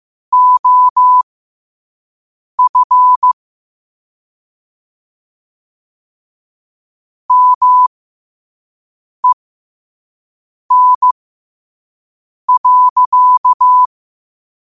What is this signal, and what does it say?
Morse code, heard as a keyed tone, about 15 wpm, spaced as for 5 wpm, 1000 Hz. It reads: OF MEN.
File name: qotdfiles / 5wpm